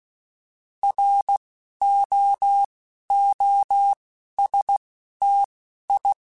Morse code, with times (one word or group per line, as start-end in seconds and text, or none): 0.8-1.4 R
1.8-2.6 O
3.1-3.9 O
4.4-4.8 S
5.2-5.4 T
5.9-6.1 I